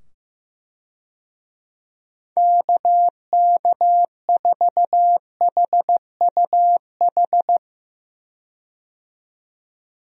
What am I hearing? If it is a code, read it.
Morse code, heard as a keyed tone, about 15 wpm, 700 Hz. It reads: KK4HUH
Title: USA Call Signs - 6 Characters - 15wpm